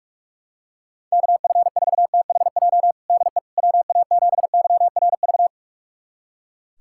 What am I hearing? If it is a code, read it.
Morse code, heard as a keyed tone, about 42 wpm, 700 Hz. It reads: KF4THJ DE WA7YRV
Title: USA Callsign DE Callsign 45wpm